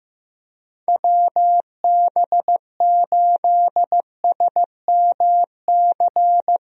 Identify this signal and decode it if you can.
Morse code, heard as a keyed tone, about 16 wpm, 700 Hz. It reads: WB8SMC
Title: USA Call Signs - 6 Characters - Repeated 2x - 15wpm